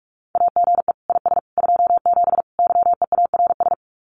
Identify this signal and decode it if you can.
Morse code, heard as a keyed tone, about 46 wpm, 700 Hz. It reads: AGE IS 27 YEARS
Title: Common QSO Phrases Encore 45wpm